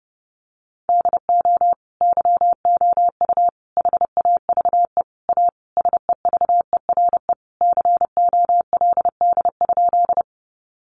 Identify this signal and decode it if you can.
Morse code, heard as a keyed tone, about 30 wpm, 700 Hz. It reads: DO YOU HAVE A SEVERE COLD?
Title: Sentences from Top 1400 Words 30wpm